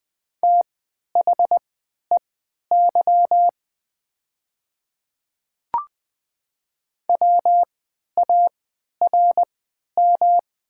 Morse code, heard as a keyed tone, about 20 wpm, spaced as for 11 wpm, 700 Hz. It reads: THEY WARM
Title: 3x Character Spacing - Top 500 Words - 20wpm